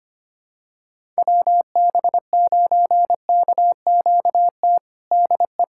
Morse code, heard as a keyed tone, about 25 wpm, 700 Hz. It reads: WB9KQT DE
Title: USA Callsign DE Callsign - 6 Characters - 25wpm